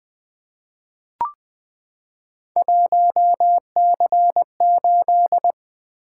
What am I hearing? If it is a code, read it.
Morse code, heard as a keyed tone, about 20 wpm, 700 Hz. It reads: 1C8
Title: Three Letter-Number Combinations - 20wpm